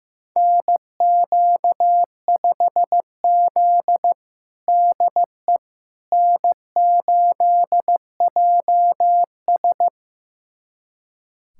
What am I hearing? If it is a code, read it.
Morse code, heard as a keyed tone, about 15 wpm, 700 Hz. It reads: NQ5Z DE N8JS
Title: USA Callsign DE Callsign - 4 Characters - 15wpm